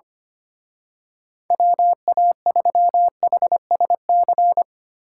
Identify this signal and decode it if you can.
Morse code, heard as a keyed tone, about 26 wpm, 700 Hz. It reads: WA3HSC
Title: Speed Racing - USA 2x3 Call Signs 25wpm